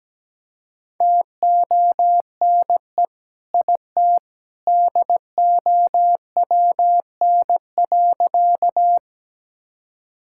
Morse code, heard as a keyed tone, about 17 wpm, 700 Hz. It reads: TONE IT DOWN.
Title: Sentences from Top 1600 Words 17wpm